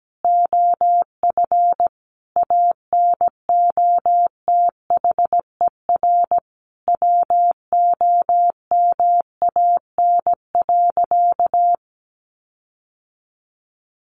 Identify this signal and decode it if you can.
Morse code, heard as a keyed tone, about 17 wpm, 700 Hz. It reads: OF ANOTHER WOMAN.
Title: Sentences from Top 1400 Words 17wpm